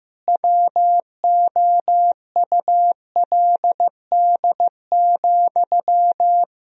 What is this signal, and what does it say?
Morse code, heard as a keyed tone, about 16 wpm, 700 Hz. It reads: WOULD,
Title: Sentences from Top 1000 Words - Review Part 2 - 15wpm